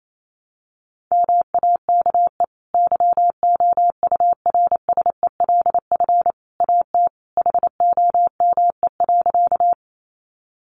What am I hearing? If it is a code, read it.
Morse code, heard as a keyed tone, about 28 wpm, 700 Hz. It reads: MAKE YOURSELF AT HOME.
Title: Sentences from Top 600 Words 28wpm